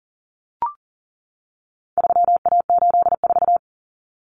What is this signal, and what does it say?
Morse code, heard as a keyed tone, about 40 wpm, 700 Hz. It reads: 3A84
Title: Common Cut Numbers - 4-Digit 40wpm